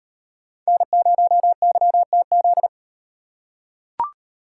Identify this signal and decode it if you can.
Morse code, heard as a keyed tone, about 35 wpm, 700 Hz. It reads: N0YTZ